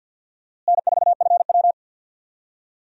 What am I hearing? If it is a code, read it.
Morse code, heard as a keyed tone, about 52 wpm, 700 Hz. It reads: N4FW
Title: USA Call Signs - 4 Characters - Repeated 3x - 50wpm